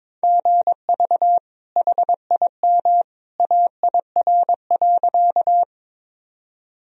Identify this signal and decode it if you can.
Morse code, heard as a keyed tone, about 22 wpm, 700 Hz. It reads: GV HIM AIR.
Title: Sentences from Top 200 Words and Abbreviations 22wpm